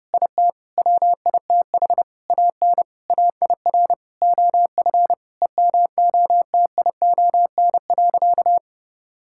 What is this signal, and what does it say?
Morse code, heard as a keyed tone, about 32 wpm, 700 Hz. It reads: IT WITH AN AIR OF EMOTION.
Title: Sentences from Top 1600 Words 30wpm